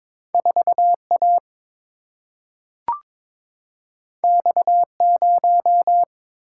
Morse code, heard as a keyed tone, about 23 wpm, 700 Hz. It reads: 4A X0
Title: Single-Letter Single-Number 22wpm